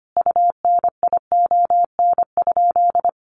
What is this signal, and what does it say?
Morse code, heard as a keyed tone, about 25 wpm, 700 Hz. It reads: UNION?
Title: Sentences from Top 1400 Words 25wpm